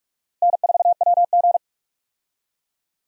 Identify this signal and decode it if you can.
Morse code, heard as a keyed tone, about 46 wpm, 700 Hz. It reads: N4WG
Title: USA Call Signs - 4 Characters - Repeated 2x - 45wpm